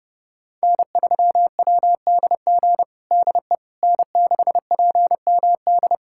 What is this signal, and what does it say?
Morse code, heard as a keyed tone, about 32 wpm, 700 Hz. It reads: N3WDG DE N6PMD
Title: USA Callsign DE Callsign - 5 Characters - 30wpm